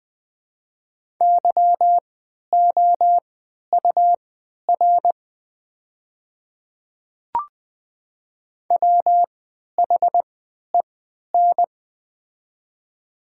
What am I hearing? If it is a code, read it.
Morse code, heard as a keyed tone, about 20 wpm, spaced as for 11 wpm, 700 Hz. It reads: YOUR WHEN
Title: 3x Character Spacing - Top 100 Words - 20wpm